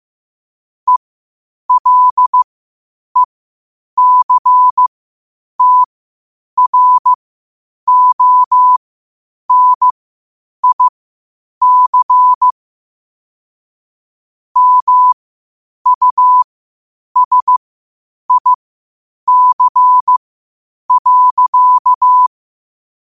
Morse code, heard as a keyed tone, about 15 wpm, spaced as for 8 wpm, 1000 Hz. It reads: ELECTRONIC MUSIC.
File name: qotdfiles / 8wpm